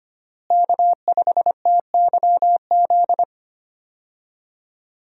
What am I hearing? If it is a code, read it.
Morse code, heard as a keyed tone, about 25 wpm, 700 Hz. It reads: K5TYZ
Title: USA Call Signs - 5 Characters - Repeated 3x - 25wpm